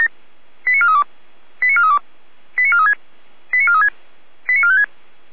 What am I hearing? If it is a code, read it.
Five-tone selective call (ZVEI), heard as quick runs of five tones, tones of about 70 ms. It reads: 89432 89432 89436 89436 89456